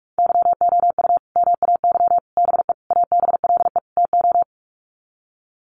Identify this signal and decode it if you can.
Morse code, heard as a keyed tone, about 44 wpm, 700 Hz. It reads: YOU MAY BE ABLE TO
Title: Sets of 5 Words Encore 45wpm